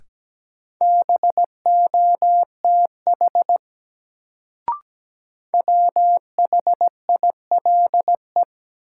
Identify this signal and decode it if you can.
Morse code, heard as a keyed tone, about 17 wpm, 700 Hz. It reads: BOTH WHILE
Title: Sentences from Top 300 Words - New Word Drill - 17wpm